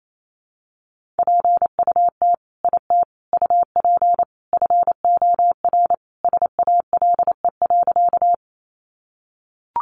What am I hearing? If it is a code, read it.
Morse code, heard as a keyed tone, about 28 wpm, 700 Hz. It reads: PUT IT UP FOR SALE.
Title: Sentences from Top 1400 Words 28wpm